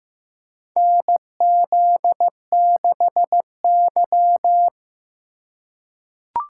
Morse code, heard as a keyed tone, about 15 wpm, 700 Hz. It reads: NZ6Y
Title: USA Call Signs - 4 Characters - 2x Character Spacing - 15wpm